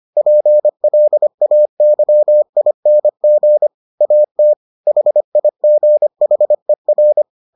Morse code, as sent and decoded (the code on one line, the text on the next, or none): .--. .-.. .- -.-- .. -. --. / .- - / .... .. --. .... .-.
PLAYING AT HIGHER